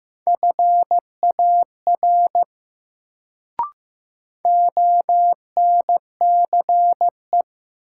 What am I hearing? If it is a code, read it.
Morse code, heard as a keyed tone, about 15 wpm, 700 Hz. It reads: FAR ONCE
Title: Sentences from Top 300 Words - New Word Drill - 15wpm